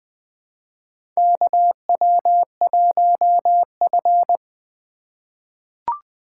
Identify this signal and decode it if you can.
Morse code, heard as a keyed tone, about 21 wpm, 700 Hz. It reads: KW1F
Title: USA Call Signs - 4 Characters - 2x Character Spacing - 20wpm